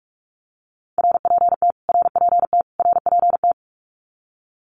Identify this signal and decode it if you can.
Morse code, heard as a keyed tone, about 44 wpm, 700 Hz. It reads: RPT RPT RPT
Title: Abbreviations Repeated 3x - 45wpm